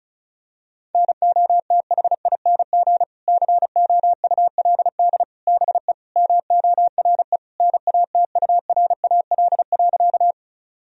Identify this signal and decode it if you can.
Morse code, heard as a keyed tone, about 36 wpm, 700 Hz. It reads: NOTHING COULD BE MORE NATURAL.
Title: Sentences from Top 1000 Words - Review Part 4 - 35wpm